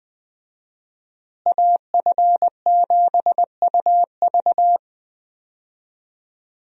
Morse code, heard as a keyed tone, about 20 wpm, 700 Hz. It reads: AF7UV
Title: USA Call Signs - 5 Characters - Repeated 2x - 20wpm